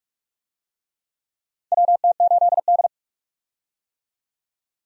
Morse code, heard as a keyed tone, about 44 wpm, 700 Hz. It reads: WT8D
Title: USA Call Signs - 4 Characters - 45wpm